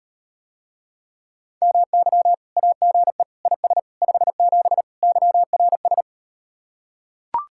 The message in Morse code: MY AGE IS 57 YRS

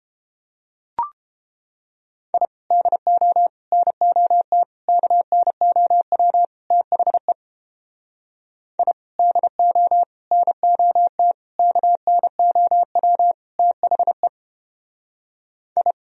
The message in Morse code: I DO NOT KNOW THE I DO NOT KNOW THE I

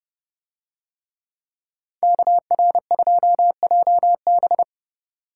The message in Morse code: KR2JB

